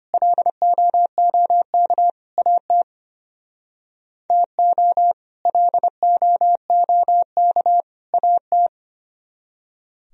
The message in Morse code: LOOK AT TO LOOK AT

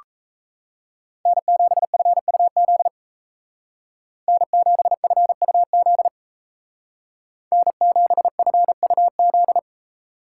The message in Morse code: N7FUZ N7FUZ N7FUZ

